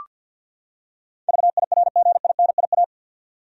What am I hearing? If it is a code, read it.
Morse code, heard as a keyed tone, about 47 wpm, 700 Hz. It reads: VIRGINIA
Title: USA Common Names - 50wpm